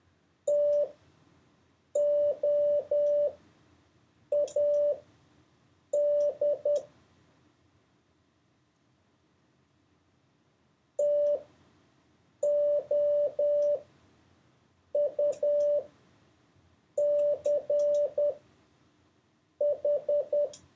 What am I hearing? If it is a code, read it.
Morse code, heard as a keyed tone, about 10 wpm, spaced as for 6 wpm, 580 Hz. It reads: TOAD TOUCH
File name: 4of6-words-1